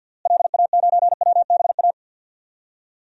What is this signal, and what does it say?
Morse code, heard as a keyed tone, about 51 wpm, 700 Hz. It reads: LA9WBA
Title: DXCC-100 Call Signs - 6 Characters - Repeated 3x - 50wpm